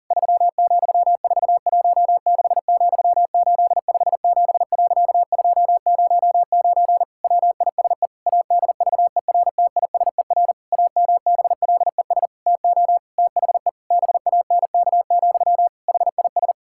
Morse code, heard as a keyed tone, about 40 wpm, 700 Hz. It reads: 3,416,857.209 WISE ADVERTISER AMBLES TO THE BANK, HIS
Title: Pangram Challenge v2 - 40wpm